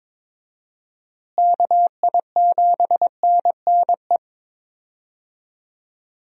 Morse code, heard as a keyed tone, about 22 wpm, 700 Hz. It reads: KI7NNE
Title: USA Call Signs - 6 Characters - Repeated 2x - 22wpm